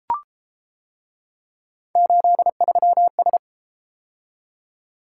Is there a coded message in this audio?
Yes: Morse code from 2.0 to 3.4 s, heard as a keyed tone, about 31 wpm, 700 Hz. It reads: 83S